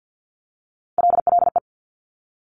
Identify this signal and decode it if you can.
Morse code, heard as a keyed tone, about 50 wpm, 700 Hz. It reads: LLE
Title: Top Three Letter Combinations in Words 50wpm